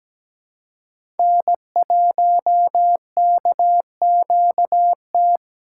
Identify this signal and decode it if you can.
Morse code, heard as a keyed tone, about 17 wpm, 700 Hz. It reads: N1KQT
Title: USA Call Signs - 5 Characters - 3x Character Spacing - 17wpm